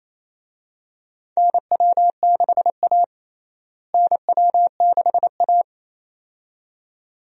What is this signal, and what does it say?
Morse code, heard as a keyed tone, about 28 wpm, 700 Hz. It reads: NW6A NW6A
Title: USA Call Signs - 4 Characters - Repeated 2x - 28wpm